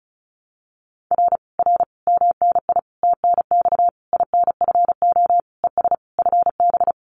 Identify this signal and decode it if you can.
Morse code, heard as a keyed tone, about 36 wpm, 700 Hz. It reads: R R MNI TNX INFO ES FB
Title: Common QSO Phrases Encore 35wpm